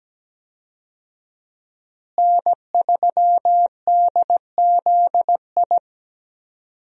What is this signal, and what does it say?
Morse code, heard as a keyed tone, about 17 wpm, 700 Hz. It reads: N3DZI
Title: USA Call Signs - 5 Characters - Repeated 2x - 17wpm